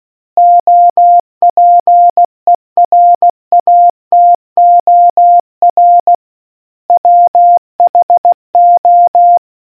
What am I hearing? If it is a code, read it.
Morse code, heard as a keyed tone, about 16 wpm, 700 Hz. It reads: OPERATOR WHO